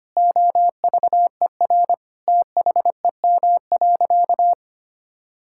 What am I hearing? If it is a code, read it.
Morse code, heard as a keyed tone, about 25 wpm, 700 Hz. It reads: OVER THEM.
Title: Sentences from Top 1000 Words - Review Part 6 - 25wpm